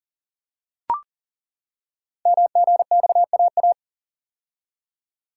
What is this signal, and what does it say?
Morse code, heard as a keyed tone, about 39 wpm, 690 Hz. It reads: MGXAA
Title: Code Groups - 5 Letters - 40wpm